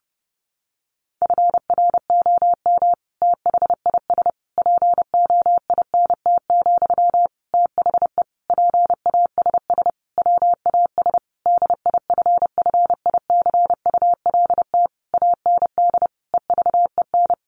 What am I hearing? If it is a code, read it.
Morse code, heard as a keyed tone, about 30 wpm, 700 Hz. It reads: FROM THIS POINT, THE PASS WAS DIFFICULT AND EVEN
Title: Sentences from Top 1200 Words 30wpm